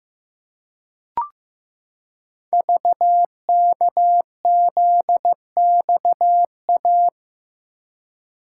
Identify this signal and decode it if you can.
Morse code, heard as a keyed tone, about 15 wpm, 700 Hz. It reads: VKZXA
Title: Code Groups - 5 Letters - 15wpm